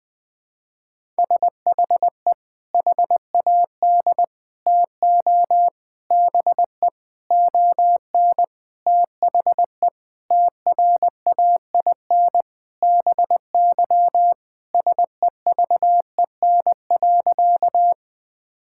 Morse code, heard as a keyed tone, about 20 wpm, 700 Hz. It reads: SHE HAD TO BE ON THE TRAIN BY SEVEN.
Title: Sentences with 4-Word Phrases - 20wpm